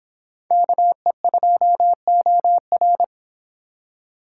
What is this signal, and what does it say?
Morse code, heard as a keyed tone, about 25 wpm, 700 Hz. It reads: KE2OR